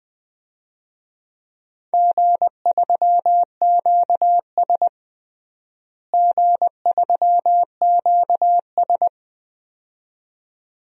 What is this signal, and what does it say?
Morse code, heard as a keyed tone, about 20 wpm, 700 Hz. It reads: G3QS G3QS
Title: DXCC-100 Call Signs - 4 Characters - Repeated 2x - 20wpm